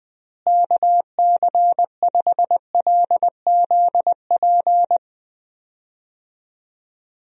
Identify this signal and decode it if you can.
Morse code, heard as a keyed tone, about 21 wpm, 700 Hz. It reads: KC5LZP
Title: USA Call Signs - 6 Characters - Repeated 2x - 20wpm